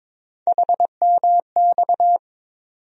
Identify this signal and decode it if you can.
Morse code, heard as a keyed tone, about 22 wpm, 700 Hz. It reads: HMX
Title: USA Call Sign Postfixes - 3 Letters 22wpm